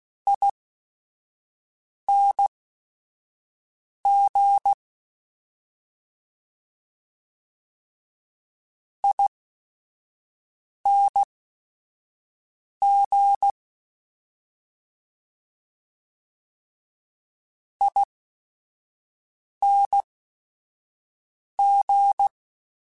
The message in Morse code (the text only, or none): ING ING ING